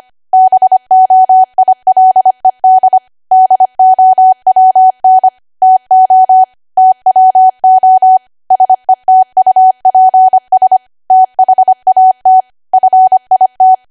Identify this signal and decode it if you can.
Morse code, heard as a keyed tone, about 25 wpm, 750 Hz. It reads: BOILED DOWN TO TWO SETUPS THAT FIT
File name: Archive / 25 WPM / 25WPM